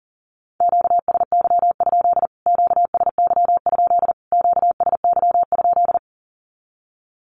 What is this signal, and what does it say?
Morse code, heard as a keyed tone, about 40 wpm, 700 Hz. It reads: QSY? QSY? QSY?
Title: Q-Codes Repeated 3x 40wpm